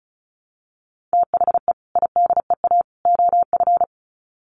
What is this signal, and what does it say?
Morse code, heard as a keyed tone, about 34 wpm, 700 Hz. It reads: THE IDEA OF
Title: Sets of 3 Words 35wpm